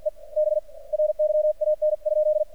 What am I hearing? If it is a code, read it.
Morse code, heard as a keyed tone, about 47 wpm, 610 Hz. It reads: E B RYAN?
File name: archived / CRSnet